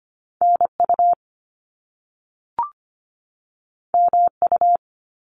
Morse code, heard as a keyed tone, about 25 wpm, 700 Hz. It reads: NU MU
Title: Top Two Letter Combinations in Words 25wpm